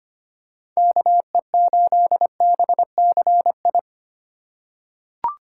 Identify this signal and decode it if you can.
Morse code, heard as a keyed tone, about 25 wpm, 700 Hz. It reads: KE8BCI
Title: USA Call Signs - 6 Characters - 4x Character Spacing - 25wpm